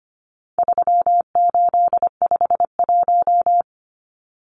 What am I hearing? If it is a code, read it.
Morse code, heard as a keyed tone, about 25 wpm, 700 Hz. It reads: 3851
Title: Numbers - 4-Digit 25wpm